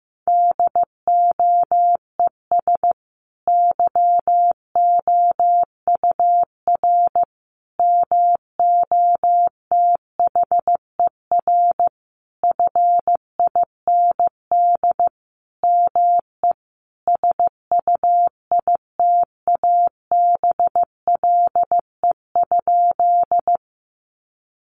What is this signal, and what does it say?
Morse code, heard as a keyed tone, about 15 wpm, 700 Hz. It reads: DOES YOUR MOTHER FIND ME SUITABLE?
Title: Sentences from Top 1400 Words 15wpm